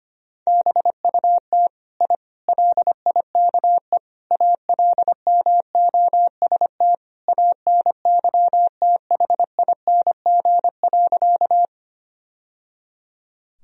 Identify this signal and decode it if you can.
Morse code, heard as a keyed tone, about 25 wpm, 700 Hz. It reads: BUT I LIKE ALMOST ANYTHING.
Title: Sentences from Top 1000 Words - Review Part 4 - 25wpm